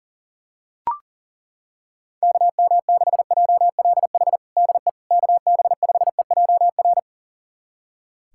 Morse code, heard as a keyed tone, about 37 wpm, 700 Hz. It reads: KM6JLH DE KB5EJR
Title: USA Callsign DE Callsign - 6 Characters - 40wpm